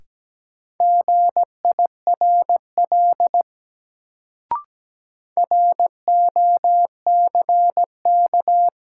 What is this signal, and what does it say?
Morse code, heard as a keyed tone, about 17 wpm, 700 Hz. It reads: GIRL ROCK
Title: Sentences from Top 400 Words - New Word Drill - 17wpm